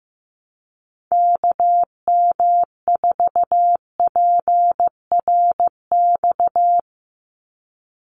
Morse code, heard as a keyed tone, about 15 wpm, 700 Hz. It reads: KM4PRX
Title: USA Call Signs 15wpm